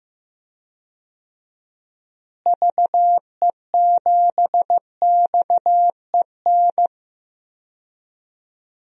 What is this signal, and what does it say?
Morse code, heard as a keyed tone, about 15 wpm, 700 Hz. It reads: VE7XEN
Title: DXCC-100 Call Signs - 6 Characters - Repeated 2x - 15wpm